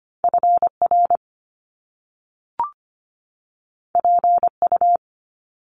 Morse code, heard as a keyed tone, about 25 wpm, 700 Hz. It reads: FR PU
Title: Top Two Letter Combinations in Words 25wpm